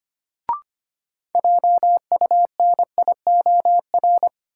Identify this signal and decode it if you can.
Morse code, heard as a keyed tone, about 25 wpm, 700 Hz. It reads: JUNIOR
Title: Sentences from Top 1000 Words - Review Part 7 - New Word Drill - 25wpm